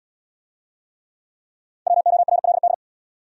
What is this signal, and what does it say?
Morse code, heard as a keyed tone, about 73 wpm, 700 Hz. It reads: WORLD